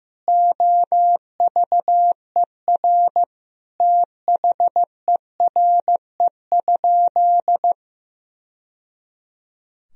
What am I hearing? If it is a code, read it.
Morse code, heard as a keyed tone, about 16 wpm, 700 Hz. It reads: OVER THERE?
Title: Sentences from Top 1000 Words - Review Part 2 - 15wpm